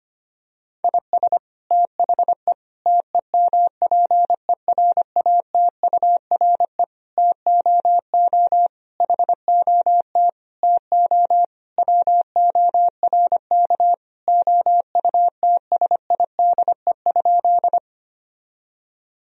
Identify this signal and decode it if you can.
Morse code, heard as a keyed tone, about 25 wpm, 700 Hz. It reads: IS THE TEMPERATURE TOO HOT TO WORK OUTSIDE?
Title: Sentences from Top 1000 Words - Review Part 4 - 25wpm